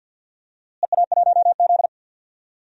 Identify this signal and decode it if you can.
Morse code, heard as a keyed tone, about 51 wpm, 700 Hz. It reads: EA1Z
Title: DXCC-100 Call Signs - 4 Characters - Repeated 3x - 50wpm